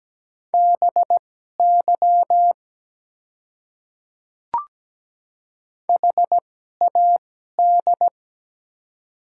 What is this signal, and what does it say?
Morse code, heard as a keyed tone, about 17 wpm, spaced as for 12 wpm, 700 Hz. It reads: BY HAD
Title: 2x Character Spacing - Top 200 Words - 17wpm